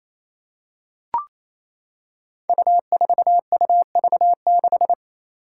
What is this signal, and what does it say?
Morse code, heard as a keyed tone, about 27 wpm, 700 Hz. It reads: U4UV6
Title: Five Challenging Letter-Number Combinations 28wpm